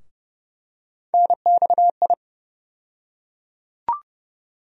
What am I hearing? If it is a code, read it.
Morse code, heard as a keyed tone, about 30 wpm, 700 Hz. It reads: NXI
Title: USA Call Sign Postfixes - 3 Letters 30wpm